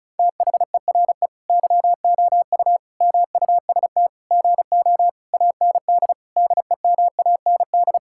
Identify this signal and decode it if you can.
Morse code, heard as a keyed tone, about 36 wpm, 700 Hz. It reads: THERE YOU MUST GO AND DEMAND
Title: Sentences from Top 1600 Words 35wpm